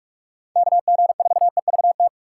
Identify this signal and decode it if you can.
Morse code, heard as a keyed tone, about 46 wpm, 700 Hz. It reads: KG4EVT
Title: USA Call Signs - 4x Character Spacing - 45wpm